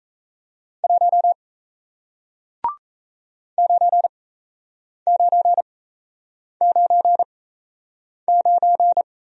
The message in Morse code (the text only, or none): 1 9999